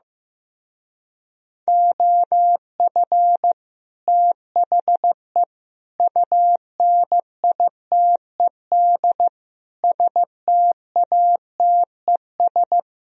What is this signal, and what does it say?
Morse code, heard as a keyed tone, about 16 wpm, 700 Hz. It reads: OF THE UNITED STATES
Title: Speed-Racing - Sets of 4 Words 15wpm